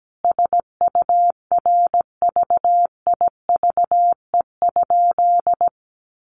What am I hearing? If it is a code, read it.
Morse code, heard as a keyed tone, about 17 wpm, 700 Hz. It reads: SURVIVE?
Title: Sentences from Top 1300 Words 17wpm